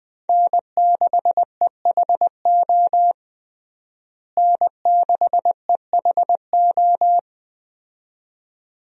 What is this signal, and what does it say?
Morse code, heard as a keyed tone, about 21 wpm, 700 Hz. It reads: N6EHO N6EHO
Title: USA Call Signs - 5 Characters - Repeated 2x - 20wpm